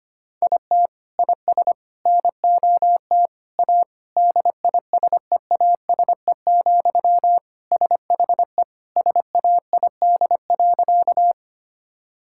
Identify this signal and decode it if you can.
Morse code, heard as a keyed tone, about 25 wpm, 700 Hz. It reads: IT IS NOT A DISEASE, SHE SAID.
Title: Sentences from Top 1000 Words - Review Part 6 - 25wpm